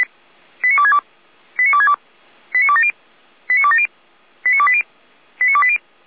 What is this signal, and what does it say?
Five-tone selective call (ZVEI), heard as quick runs of five tones, tones of about 70 ms. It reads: 89262 89262 89270 89270 89280 89280
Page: SDR/ZVEI